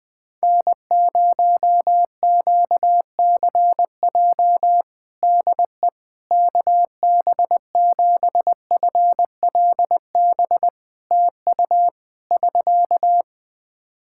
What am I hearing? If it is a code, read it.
Morse code, heard as a keyed tone, about 20 wpm, 700 Hz. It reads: N0QCJ DE KB7FLB TU <SK>
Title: Practice QSO - Exchange 4 - Ending - 20wpm